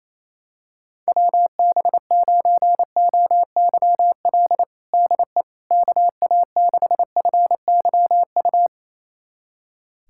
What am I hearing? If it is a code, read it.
Morse code, heard as a keyed tone, about 28 wpm, 700 Hz. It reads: WB9OYL DE KA6FYU
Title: USA Callsign DE Callsign - 6 Characters - 28wpm